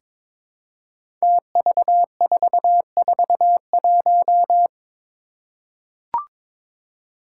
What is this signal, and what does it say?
Morse code, heard as a keyed tone, about 22 wpm, 700 Hz. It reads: TV441
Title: Five Challenging Letter-Number Combinations 22wpm